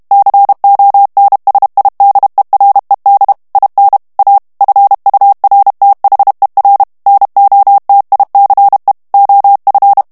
Morse code, heard as a keyed tone, about 32 wpm, 800 Hz. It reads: CONSIDERED IN A FURTHER NOTICE OF
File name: ANGLAIS texte clair 3 vitesse 30 mots minute